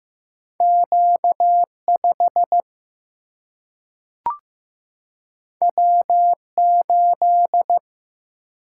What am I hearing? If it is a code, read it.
Morse code, heard as a keyed tone, about 15 wpm, 700 Hz. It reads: Q5 W8
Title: Single-Letter Single-Number 15wpm